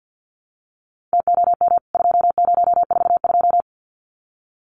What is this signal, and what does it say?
Morse code, heard as a keyed tone, about 50 wpm, 700 Hz. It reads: TOM 2042
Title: CWT Exchange 50wpm